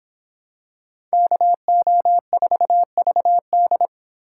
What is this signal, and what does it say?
Morse code, heard as a keyed tone, about 25 wpm, 700 Hz. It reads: KO4VD